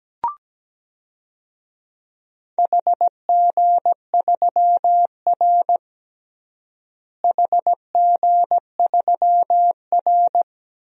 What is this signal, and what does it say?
Morse code, heard as a keyed tone, about 17 wpm, 700 Hz. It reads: HG3R HG3R
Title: DXCC-100 Call Signs - 4 Characters - Repeated 2x - 17wpm